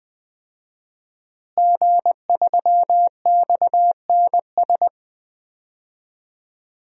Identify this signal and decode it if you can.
Morse code, heard as a keyed tone, about 20 wpm, 700 Hz. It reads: G3XNS
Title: DXCC-100 Call Signs - 5 Characters - Repeated 3x - 20wpm